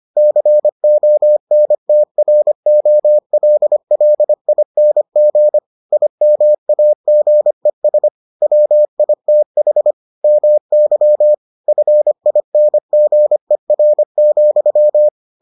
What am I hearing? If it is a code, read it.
Morse code, heard as a keyed tone, about 25 wpm, 600 Hz. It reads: CONTROLLING IMAGES WITH MY FINGER,